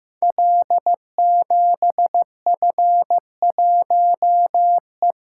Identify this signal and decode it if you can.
Morse code, heard as a keyed tone, about 15 wpm, 700 Hz. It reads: L7F1E
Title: Code Groups - 5 Letter:Numbers - 15wpm